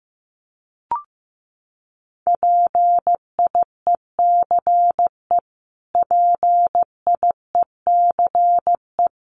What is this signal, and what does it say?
Morse code, heard as a keyed tone, about 14 wpm, 700 Hz. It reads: PIECE PIECE
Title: Top 500 Words Repeated 15wpm